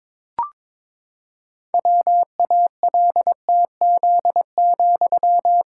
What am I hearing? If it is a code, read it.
Morse code, heard as a keyed tone, about 22 wpm, 700 Hz. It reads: WALTZ,
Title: Pangram Challenge v2 - 22wpm